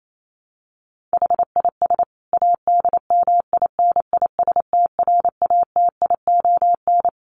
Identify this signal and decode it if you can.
Morse code, heard as a keyed tone, about 28 wpm, 700 Hz. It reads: HIS ADMINISTRATION
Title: Sentences from Top 1300 Words 28wpm